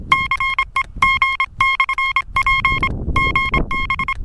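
Morse code, heard as a keyed tone, about 26 wpm, 2100 Hz, with noise. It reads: CEG/P GB
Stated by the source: QSO between IK4WKU and UA1CEG/p, a portable QRP contact on home-made gear, UA1CEG operating in the forest.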